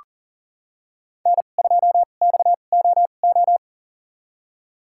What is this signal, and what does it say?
Morse code, heard as a keyed tone, about 40 wpm, spaced as for 29 wpm, 700 Hz. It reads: N2XOO
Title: USA Call Signs - 5 Characters - 2x Character Spacing - 40wpm